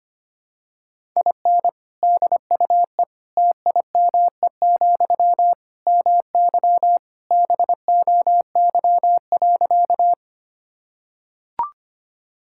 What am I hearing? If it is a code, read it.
Morse code, heard as a keyed tone, about 25 wpm, 700 Hz. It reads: IN DUE TIME, MY BOY.
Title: Sentences from Top 1000 Words - Review Part 6 - 25wpm